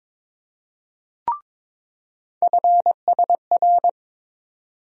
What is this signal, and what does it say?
Morse code, heard as a keyed tone, about 22 wpm, 700 Hz. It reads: FSR
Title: USA Call Sign Postfixes - 3 Letters 22wpm